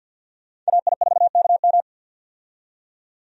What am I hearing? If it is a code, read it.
Morse code, heard as a keyed tone, about 48 wpm, 700 Hz. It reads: AI4KM
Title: USA Call Signs - 5 Characters - 50wpm